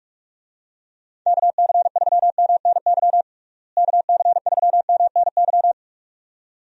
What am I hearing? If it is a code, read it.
Morse code, heard as a keyed tone, about 42 wpm, 700 Hz. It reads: KC3MNY KC3MNY
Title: USA Call Signs - 6 Characters - Repeated 2x - 45wpm